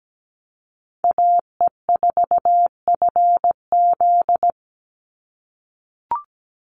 Morse code, heard as a keyed tone, about 17 wpm, 700 Hz. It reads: AE4FZ
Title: NA SOTA Chaser Callsigns 17wpm